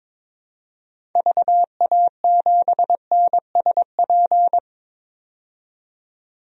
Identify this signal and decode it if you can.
Morse code, heard as a keyed tone, about 22 wpm, 700 Hz. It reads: VA7NSP
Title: DXCC-100 Call Signs - 6 Characters - 22wpm